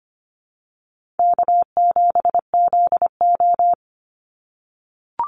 Morse code, heard as a keyed tone, about 25 wpm, 700 Hz. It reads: K7ZO
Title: NA SOTA Chaser Callsigns 25wpm